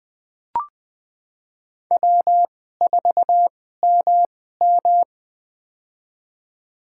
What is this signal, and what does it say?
Morse code, heard as a keyed tone, about 20 wpm, spaced as for 14 wpm, 700 Hz. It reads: W4MM
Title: USA Call Signs - 2x Character Spacing - 20wpm